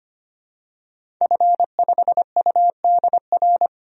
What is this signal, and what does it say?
Morse code, heard as a keyed tone, about 25 wpm, 700 Hz. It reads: F5UDR
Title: DXCC-100 Call Signs - 5 Characters - 25wpm